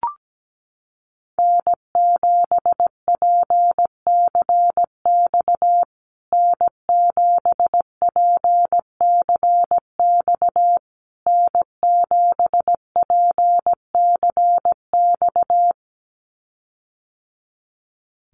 Morse code, heard as a keyed tone, about 17 wpm, 700 Hz. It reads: N7PCX N7PCX N7PCX
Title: USA Call Signs Repeated 3x 17wpm